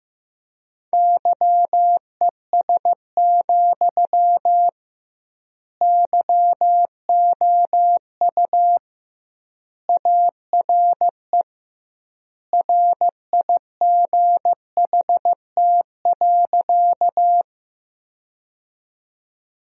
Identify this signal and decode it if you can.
Morse code, heard as a keyed tone, about 16 wpm, 700 Hz. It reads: YES, YOU ARE RIGHT.
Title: 2x Word-Spacing - Sentences from Top 500 Words - 15wpm